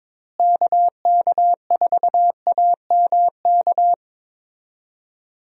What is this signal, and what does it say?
Morse code, heard as a keyed tone, about 21 wpm, 700 Hz. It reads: KK4AMK
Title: USA Call Signs - 6 Characters - Repeated 3x - 22wpm